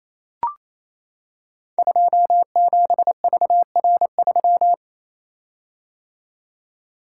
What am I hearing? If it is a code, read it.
Morse code, heard as a keyed tone, about 28 wpm, 690 Hz. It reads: 27VR3
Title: Code Groups - 5 Letter:Numbers - 28wpm